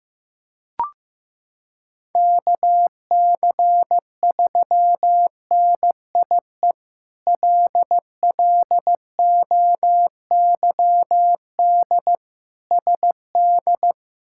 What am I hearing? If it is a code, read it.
Morse code, heard as a keyed tone, about 15 wpm, 700 Hz. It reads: KC3NIE LLOYD SD
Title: K1USN SST QSO Practice - 15wpm